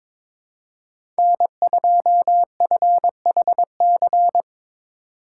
Morse code, heard as a keyed tone, about 22 wpm, 700 Hz. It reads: N2FHC
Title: USA Call Signs - 5 Characters - 2x Character Spacing - 22wpm